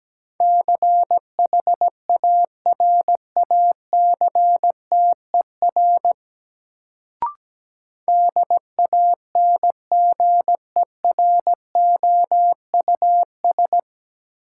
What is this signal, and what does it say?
Morse code, heard as a keyed tone, about 17 wpm, 700 Hz. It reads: CHARACTER DANGEROUS
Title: Sentences from Top 1200 Words - New Word Drill - 17wpm